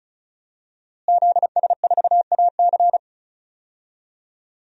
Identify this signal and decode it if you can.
Morse code, heard as a keyed tone, about 34 wpm, 700 Hz. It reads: ZS4AC
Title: DXCC-100 Call Signs - 5 Characters - Repeated 2x - 35wpm